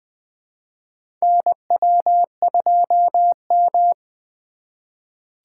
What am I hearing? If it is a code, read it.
Morse code, heard as a keyed tone, about 21 wpm, 700 Hz. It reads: NW2M
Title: USA Call Signs - 4 Characters - 3x Character Spacing - 20wpm